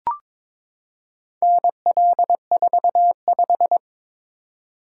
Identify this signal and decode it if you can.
Morse code, heard as a keyed tone, about 22 wpm, 690 Hz. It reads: NL45
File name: practice / 22wpm / Four Letter-Number Combinations - 22wpm